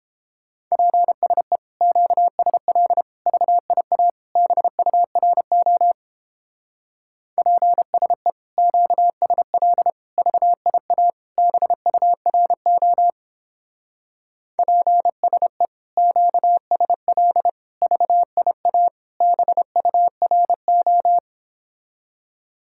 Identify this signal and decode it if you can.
Morse code, heard as a keyed tone, about 31 wpm, 690 Hz. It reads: PSE QSL VIA BURO PSE QSL VIA BURO PSE QSL VIA BURO